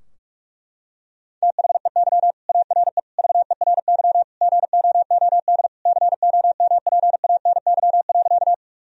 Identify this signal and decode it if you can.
Morse code, heard as a keyed tone, about 42 wpm, 700 Hz. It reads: THEY ARE VERY GOOD COMPANY.
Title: Sentences from Top 1000 Words - Review Part 3 - 45wpm